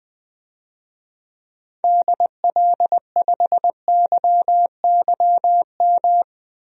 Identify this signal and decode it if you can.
Morse code, heard as a keyed tone, about 20 wpm, 700 Hz. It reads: DL5YYM
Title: DXCC-100 Call Signs - 6 Characters - 20wpm